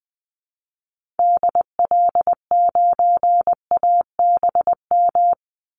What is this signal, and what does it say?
Morse code, heard as a keyed tone, about 20 wpm, 700 Hz. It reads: DL9ABM
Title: Call Signs - 20wpm